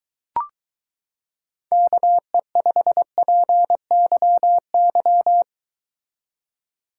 Morse code, heard as a keyed tone, about 23 wpm, 700 Hz. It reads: KE5PYY